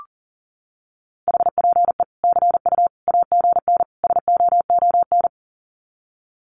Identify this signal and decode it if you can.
Morse code, heard as a keyed tone, about 40 wpm, 700 Hz. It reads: HPE CU AGN SOON
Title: Common QSO Phrases 40wpm